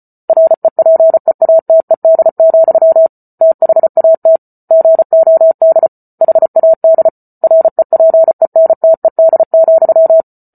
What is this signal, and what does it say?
Morse code, heard as a keyed tone, about 34 wpm, 650 Hz. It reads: REPEATED, THAT GOD HAD REPENTED,